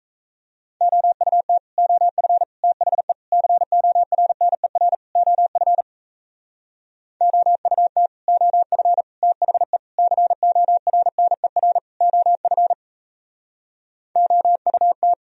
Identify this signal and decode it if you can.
Morse code, heard as a keyed tone, about 41 wpm, 700 Hz. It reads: OUT OF THE CORNER OF OUT OF THE CORNER OF OUT